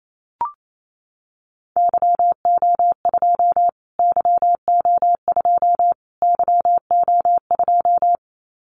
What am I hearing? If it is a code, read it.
Morse code, heard as a keyed tone, about 28 wpm, 700 Hz. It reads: YO2 YO2 YO2
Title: Callsign Prefixes for Top 25 DXCC - Repeated 3x - 28wpm